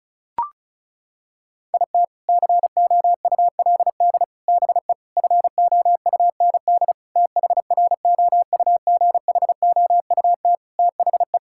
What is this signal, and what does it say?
Morse code, heard as a keyed tone, about 34 wpm, 700 Hz. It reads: IT COULD BE FOUND THROUGHOUT THE
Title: Sentences from Top 1000 Words - Review Part 5 - 35wpm